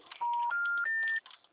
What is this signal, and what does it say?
Special information tone.
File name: out- -2009- - - .3002